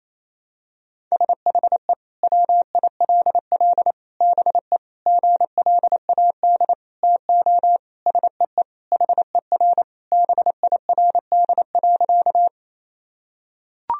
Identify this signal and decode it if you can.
Morse code, heard as a keyed tone, about 28 wpm, 700 Hz. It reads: SHE WILL BE GLAD TO SEE HER BIRD.
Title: Sentences from Top 1000 Words - Review Part 2 - 28wpm